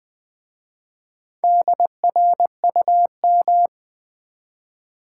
Morse code, heard as a keyed tone, about 20 wpm, 700 Hz. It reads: DRUM
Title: Easy Words with Hints - 20wpm